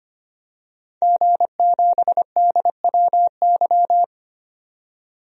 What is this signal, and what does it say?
Morse code, heard as a keyed tone, about 25 wpm, 700 Hz. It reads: G7DWY
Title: DXCC-100 Call Signs - 5 Characters - 25wpm